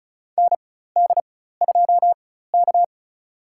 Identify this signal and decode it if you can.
Morse code, heard as a keyed tone, about 36 wpm, spaced as for 16 wpm, 700 Hz. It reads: ND2K
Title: USA Call Signs - 4 Characters - 4x Character Spacing - 35wpm